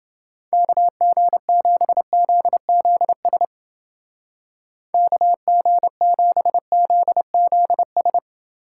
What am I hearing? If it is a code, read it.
Morse code, heard as a keyed tone, about 29 wpm, 700 Hz. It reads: KG7ZZS KG7ZZS